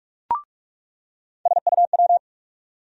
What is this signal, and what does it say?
Morse code, heard as a keyed tone, about 43 wpm, 700 Hz. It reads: SUW
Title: USA Call Sign Postfixes - 3 Letters 45wpm